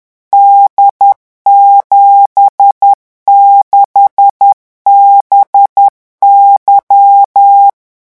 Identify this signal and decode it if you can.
Morse code, heard as a keyed tone, about 10 wpm, 800 Hz. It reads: D76BY
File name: morse mp3 / revision13-vitesse 10 mots